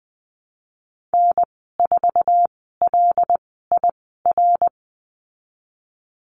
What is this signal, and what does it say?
Morse code, heard as a keyed tone, about 19 wpm, spaced as for 14 wpm, 700 Hz. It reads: N4LIR
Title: 088 Lesson 15g - Number 5 - Call Signs 20wpm